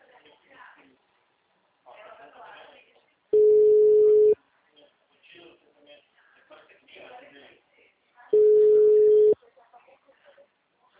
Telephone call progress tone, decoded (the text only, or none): ringback tone